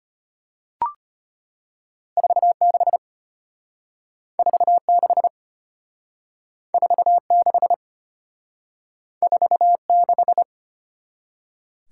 Morse code, heard as a keyed tone, about 33 wpm, 700 Hz. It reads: 46 46 46 46